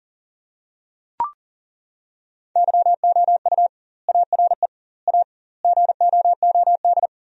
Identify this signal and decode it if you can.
Morse code, heard as a keyed tone, about 37 wpm, 700 Hz. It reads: YOU ARE A GOOD
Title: 2x Word-Spacing - Sentences from Top 400 Words - 40wpm